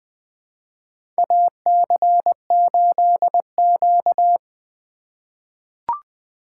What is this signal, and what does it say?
Morse code, heard as a keyed tone, about 20 wpm, 700 Hz. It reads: AC8Q
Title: USA Call Signs - 4 Characters - 2x Character Spacing - 20wpm